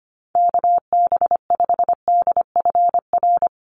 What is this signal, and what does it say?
Morse code, heard as a keyed tone, about 25 wpm, 700 Hz. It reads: KB5DFR
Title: USA Call Signs 25wpm